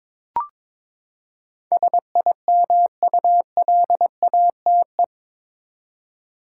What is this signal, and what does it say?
Morse code, heard as a keyed tone, about 21 wpm, 700 Hz. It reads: SIMULATE
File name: practice / 22wpm / Eight Letter Words - 22wpm